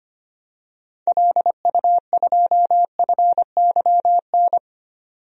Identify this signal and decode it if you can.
Morse code, heard as a keyed tone, about 25 wpm, 700 Hz. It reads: LU2FYN
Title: DXCC-100 Call Signs - 6 Characters - 25wpm